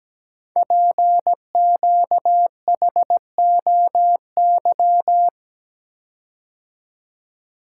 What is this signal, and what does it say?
Morse code, heard as a keyed tone, about 17 wpm, 700 Hz. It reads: PQHOY
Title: Code Groups - 5 Letters - 17wpm